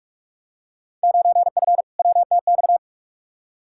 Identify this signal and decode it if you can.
Morse code, heard as a keyed tone, about 45 wpm, 700 Hz. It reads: 9F WTX
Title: Field Day Exchanges Encore - 45wpm